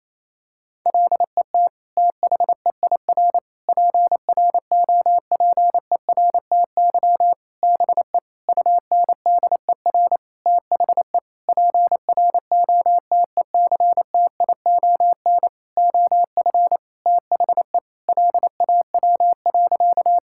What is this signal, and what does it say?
Morse code, heard as a keyed tone, about 28 wpm, 700 Hz. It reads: LET THEIR PROPERTY BE UNDER THE PROTECTION OF THE LAW.
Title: Sentences from Top 1600 Words 28wpm